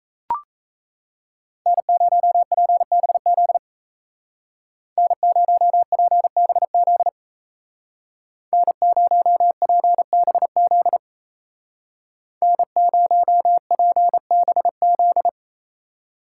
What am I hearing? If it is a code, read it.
Morse code, heard as a keyed tone, about 33 wpm, 700 Hz. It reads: N0PBZ N0PBZ N0PBZ N0PBZ
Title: Speed Racing - USA 1x3 Call Signs 28wpm